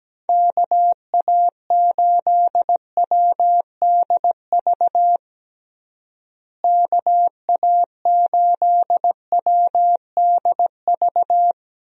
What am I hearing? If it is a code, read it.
Morse code, heard as a keyed tone, about 17 wpm, 700 Hz. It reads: KA8WDV KA8WDV
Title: USA Call Signs - 6 Characters - Repeated 3x - 17wpm